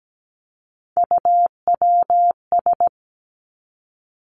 Morse code, heard as a keyed tone, about 17 wpm, 700 Hz. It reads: UWS
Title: Three Challenging Letter-Number Combinations 17wpm